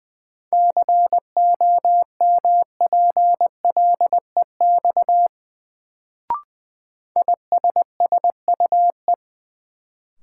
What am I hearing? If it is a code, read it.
Morse code, heard as a keyed tone, about 21 wpm, 700 Hz. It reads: COMPLEX ISSUE
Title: Sentences from Top 900 Words - New Word Drill - 20wpm